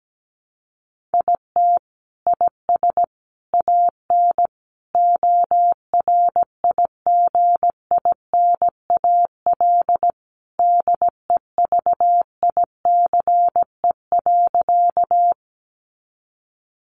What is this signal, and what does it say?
Morse code, heard as a keyed tone, about 17 wpm, 700 Hz. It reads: IT IS AN ORIGINAL DEVICE.
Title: Sentences from Top 1500 Words 17wpm